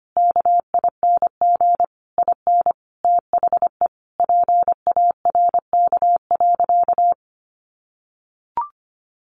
Morse code, heard as a keyed tone, about 25 wpm, 700 Hz. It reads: KING IN THE PARK.
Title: Sentences from Top 1300 Words 25wpm